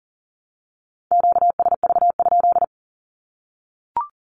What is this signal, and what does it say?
Morse code, heard as a keyed tone, about 40 wpm, 700 Hz. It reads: QSV?
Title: Q-Codes Repeated 3x 40wpm